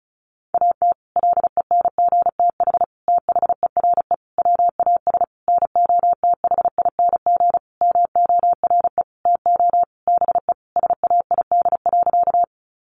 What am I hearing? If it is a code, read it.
Morse code, heard as a keyed tone, about 34 wpm, 700 Hz. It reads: AT LENGTH THERE WAS NOTHING MORE TO BE SAID.
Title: Sentences from Top 1000 Words 35wpm